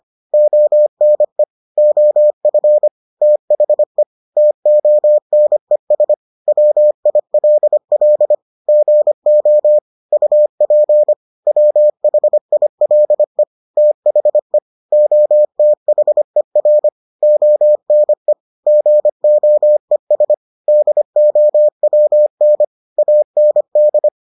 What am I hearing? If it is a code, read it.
Morse code, heard as a keyed tone, about 25 wpm, 600 Hz. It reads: ONE OF THE TONES WILL GO UP WHILE THE OTHER ONE GOES DOWN AND